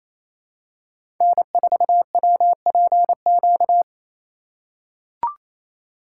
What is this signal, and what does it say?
Morse code, heard as a keyed tone, about 28 wpm, 700 Hz. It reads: N4WPQ
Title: USA Call Signs - 5 Characters - 2x Character Spacing - 28wpm